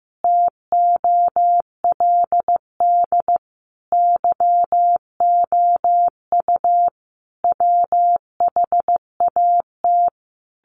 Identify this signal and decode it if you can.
Morse code, heard as a keyed tone, about 15 wpm, 700 Hz. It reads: TOLD YOU WHAT
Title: Sentences from Top 1000 Words 15wpm